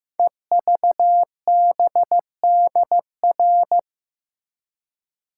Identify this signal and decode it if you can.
Morse code, heard as a keyed tone, about 15 wpm, 700 Hz. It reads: EVBDR
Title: Code Groups - 5 Letter:Numbers - 15wpm